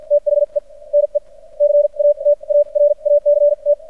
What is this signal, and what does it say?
Morse code, heard as a keyed tone, about 53 wpm, spaced as for 41 wpm, 580 Hz. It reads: THE NE CATARACT
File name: archived / CRSnet